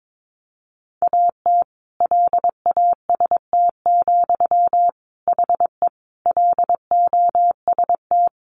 Morse code, heard as a keyed tone, about 22 wpm, 700 Hz. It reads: AT LAST, HE LOST
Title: Sentences from Top 1400 Words 22wpm